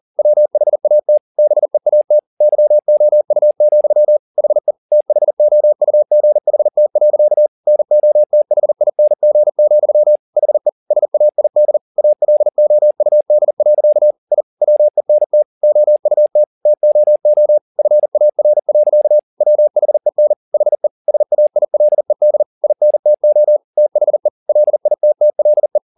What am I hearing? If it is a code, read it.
Morse code, heard as a keyed tone, about 40 wpm, 600 Hz. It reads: WHAT BEAT YOU, HE THOUGHT. NOTHING, HE SAID ALOUD. I WENT OUT TOO FAR. WHEN HE SAILED INTO THE LITTLE
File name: the old man and the sea 2 40wpm